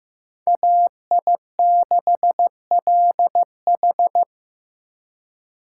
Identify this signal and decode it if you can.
Morse code, heard as a keyed tone, about 18 wpm, 700 Hz. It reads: AI6LH
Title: Speed Racing - USA 2x2 Call Signs 15wpm